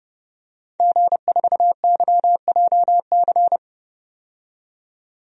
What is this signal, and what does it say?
Morse code, heard as a keyed tone, about 30 wpm, 700 Hz. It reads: G4YJC
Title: DXCC-100 Call Signs - 5 Characters - 30wpm